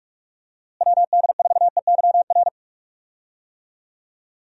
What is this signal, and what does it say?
Morse code, heard as a keyed tone, about 46 wpm, 700 Hz. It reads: WD4EYR
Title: USA Call Signs - 6 Characters - Repeated 3x - 45wpm